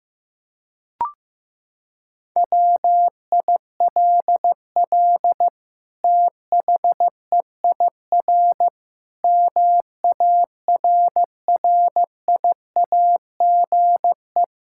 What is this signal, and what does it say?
Morse code, heard as a keyed tone, about 15 wpm, 700 Hz. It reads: WILL THEIR MARRIAGE
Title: Sentences from Top 1000 Words - Review Part 6 - 15wpm